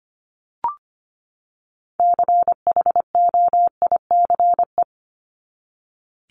Morse code, heard as a keyed tone, about 25 wpm, 700 Hz. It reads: CHOICE
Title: Top 751-1000 Words 25wpm